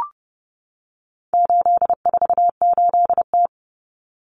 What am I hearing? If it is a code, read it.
Morse code, heard as a keyed tone, about 30 wpm, 700 Hz. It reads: 848T
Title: Common Cut Numbers - 4-Digit 30wpm